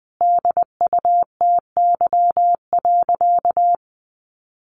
Morse code, heard as a keyed tone, about 20 wpm, 700 Hz. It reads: DUTY.
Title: Sentences from Top 1500 Words 20wpm